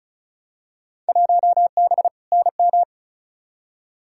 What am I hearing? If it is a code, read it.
Morse code, heard as a keyed tone, about 34 wpm, 700 Hz. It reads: 1B NM
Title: Field Day Exchanges Encore - 35wpm